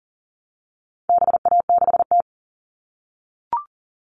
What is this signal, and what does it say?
Morse code, heard as a keyed tone, about 40 wpm, 700 Hz. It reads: BA6T
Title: All Cut Numbers - 4-Digit 40wpm